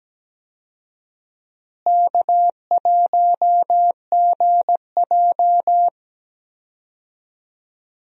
Morse code, heard as a keyed tone, about 17 wpm, 700 Hz. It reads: K1GJ
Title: USA Call Signs - 4 Characters - Repeated 2x - 17wpm